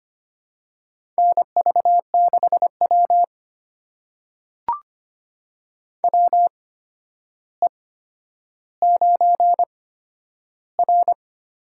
Morse code, heard as a keyed tone, about 25 wpm, spaced as for 11 wpm, 700 Hz. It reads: NV6W WE9R
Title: USA Call Signs - 4 Characters - 8x Character Spacing - 25wpm